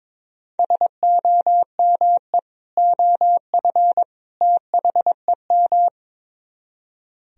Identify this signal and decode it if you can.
Morse code, heard as a keyed tone, about 22 wpm, 700 Hz. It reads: SOME OF THEM
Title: Speed-Racing - Sets of 3 Words 22wpm